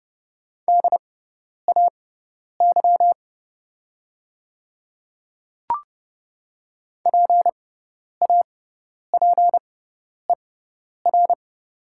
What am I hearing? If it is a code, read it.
Morse code, heard as a keyed tone, about 30 wpm, spaced as for 10 wpm, 700 Hz. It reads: DAY PAPER
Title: 6x Character Spacing - Top 300 Words - 30wpm